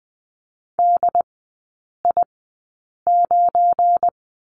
Morse code, heard as a keyed tone, about 20 wpm, spaced as for 8 wpm, 700 Hz. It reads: DI9
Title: 191 Lesson 31e - Letter v - Three Characters 20wpm